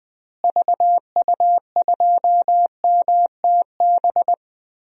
Morse code, heard as a keyed tone, about 20 wpm, 700 Hz. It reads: VU2MTB
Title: DXCC-100 Call Signs - 6 Characters - Repeated 3x - 20wpm